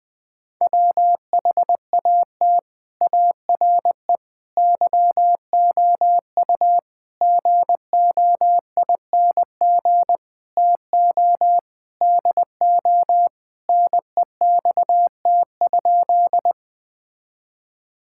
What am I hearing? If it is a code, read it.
Morse code, heard as a keyed tone, about 20 wpm, 700 Hz. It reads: WHAT ARE YOU GOING TO DO NEXT?
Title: Sentences with 5-Word Phrases - Spoken Beginning - 20wpm